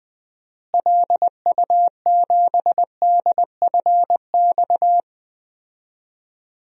Morse code, heard as a keyed tone, about 20 wpm, 690 Hz. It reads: LU7DFX
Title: DXCC-100 Call Signs - 6 Characters - Repeated 2x - 20wpm